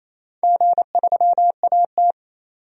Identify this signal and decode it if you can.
Morse code, heard as a keyed tone, about 28 wpm, 700 Hz. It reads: G3AT
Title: DXCC-100 Call Signs - 4 Characters - 28wpm